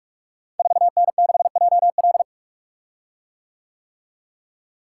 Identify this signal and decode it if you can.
Morse code, heard as a keyed tone, about 44 wpm, 690 Hz. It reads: 4N6JL
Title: Code Groups - 5 Letter:Numbers - 45wpm